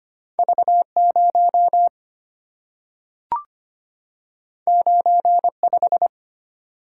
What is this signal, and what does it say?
Morse code, heard as a keyed tone, about 25 wpm, 690 Hz. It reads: V0 95
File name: Two Letter-Number Combinations - 25wpm